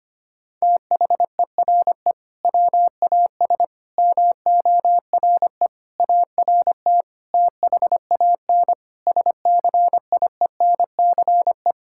Morse code, heard as a keyed tone, about 25 wpm, 700 Hz. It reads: THERE WAS MORE ART THAN SCIENCE
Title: Sentences from Top 1000 Words - Review Part 6 - 25wpm